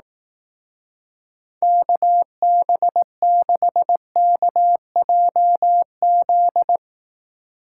Morse code, heard as a keyed tone, about 18 wpm, 700 Hz. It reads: KB6KJZ
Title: Speed Racing - USA 2x3 Call Signs 15wpm